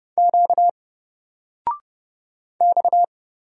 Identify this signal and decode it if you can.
Morse code, heard as a keyed tone, about 30 wpm, 700 Hz. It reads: Q X
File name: Single-Letter-Number-30wpm